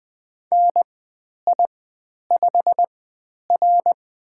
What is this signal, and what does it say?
Morse code, heard as a keyed tone, about 20 wpm, spaced as for 10 wpm, 700 Hz. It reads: NI5R
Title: 095 Lesson 16g - Letter c - Call Signs 20wpm